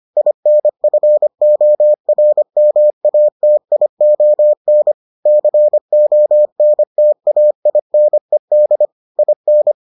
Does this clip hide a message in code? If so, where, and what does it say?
Yes: Morse code from 0.2 to 9.7 s, heard as a keyed tone, about 25 wpm, 600 Hz. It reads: INFORMATION CONTAINED IN